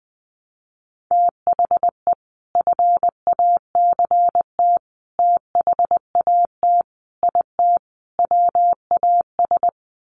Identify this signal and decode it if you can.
Morse code, heard as a keyed tone, about 20 wpm, 700 Hz. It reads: THE FACT THAT IT WAS
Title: Sets of 5 Words Encore2 20wpm